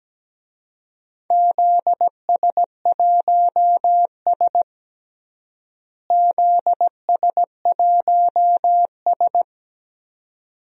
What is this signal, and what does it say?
Morse code, heard as a keyed tone, about 17 wpm, 700 Hz. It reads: ZS1S ZS1S
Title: DXCC-100 Call Signs - 4 Characters - Repeated 2x - 17wpm